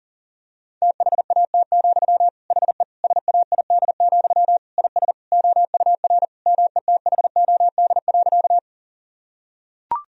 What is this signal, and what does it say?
Morse code, heard as a keyed tone, about 40 wpm, 700 Hz. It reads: THAT, HE SAID, IS OUR METHOD.
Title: Sentences from Top 1000 Words - Review Part 3 - 40wpm